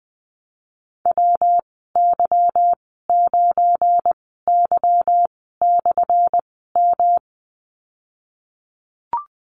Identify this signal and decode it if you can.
Morse code, heard as a keyed tone, about 19 wpm, spaced as for 14 wpm, 700 Hz. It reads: WY9Y/M
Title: 179 Lesson 29g - Slash - Call Signs 20wpm